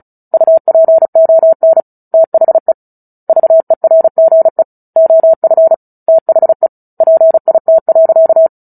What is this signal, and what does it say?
Morse code, heard as a keyed tone, about 34 wpm, 650 Hz. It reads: UPON THE VERGE OF THE PIT.
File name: Book Two / 09-Chapter 9 - 35 WPM